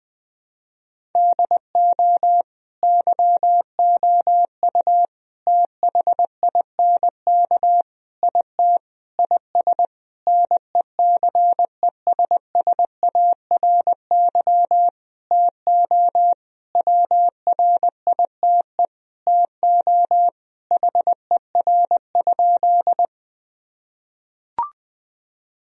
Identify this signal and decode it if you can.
Morse code, heard as a keyed tone, about 21 wpm, 700 Hz. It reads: DO YOU THINK IT IS NECESSARY TO WRITE TO HER?
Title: Sentences from Top 1000 Words - Review Part 1 - 20wpm